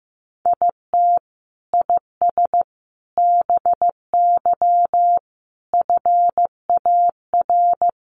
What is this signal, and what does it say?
Morse code, heard as a keyed tone, about 15 wpm, 700 Hz. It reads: IT IS BY FAR
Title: Sentences from Top 1200 Words 15wpm